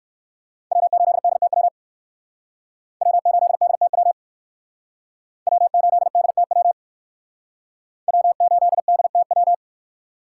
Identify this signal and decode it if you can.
Morse code, heard as a keyed tone, about 58 wpm, 690 Hz. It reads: W8DTW W8DTW W8DTW W8DTW